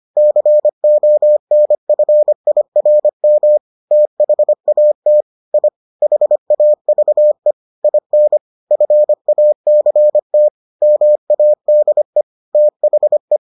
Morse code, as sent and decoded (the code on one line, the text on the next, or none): -.-. --- -. ..-. .. .-. -- / - .... .- - / .. / .... .- ...- . / .. -. / ..-. .- -.-. - / -- .- -.. / - .... .
CONFIRM THAT I HAVE IN FACT MADE THE